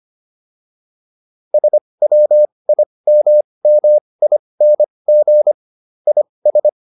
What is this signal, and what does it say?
Morse code, heard as a keyed tone, about 25 wpm, spaced as for 20 wpm, 600 Hz. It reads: SWIMMING IS